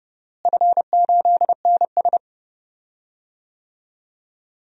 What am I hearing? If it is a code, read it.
Morse code, heard as a keyed tone, about 31 wpm, 700 Hz. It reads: F8NS
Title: DXCC-100 Call Signs - 4 Characters - Repeated 2x - 30wpm